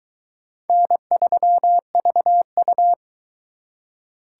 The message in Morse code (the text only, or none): N3VU